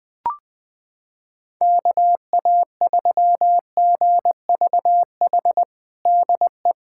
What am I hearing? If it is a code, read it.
Morse code, heard as a keyed tone, about 20 wpm, 700 Hz. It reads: KA3GVH DE
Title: USA Callsign DE Callsign - 6 Characters - 20wpm